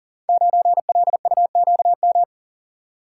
Morse code, heard as a keyed tone, about 40 wpm, 690 Hz. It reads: 9LUQM
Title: Code Groups - 5 Letter:Numbers - 40wpm